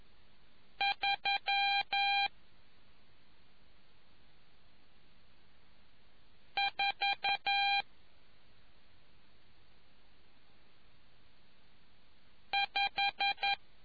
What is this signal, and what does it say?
Morse code, heard as a keyed tone, about 11 wpm, spaced as for 2 wpm, 800 Hz. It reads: 345